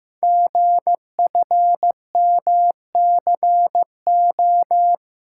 Morse code, heard as a keyed tone, about 15 wpm, 700 Hz. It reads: GFMCO
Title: Code Groups - 5 Letters - 15wpm